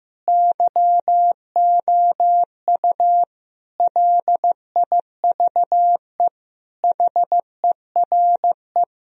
Morse code, heard as a keyed tone, about 15 wpm, 700 Hz. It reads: YOU LIVE HERE
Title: Sentences from Top 1000 Words - Review Part 1 - 15wpm